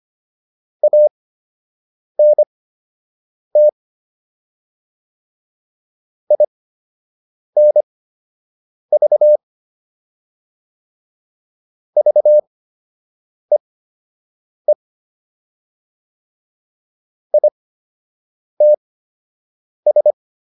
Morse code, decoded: ANT INV VEE ITS

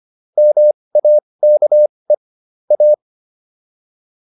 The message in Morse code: MAKE A

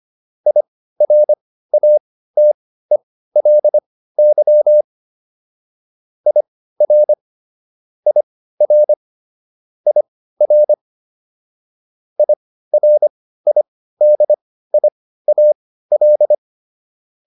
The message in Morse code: IRATELY IR IR IR IRIDIAL

